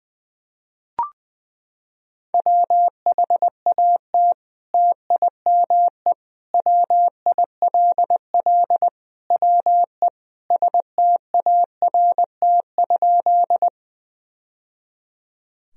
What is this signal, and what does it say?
Morse code, heard as a keyed tone, about 21 wpm, 700 Hz. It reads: WHAT TIME WILL WE START?
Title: Sentences from Top 1000 Words - Review Part 1 - 20wpm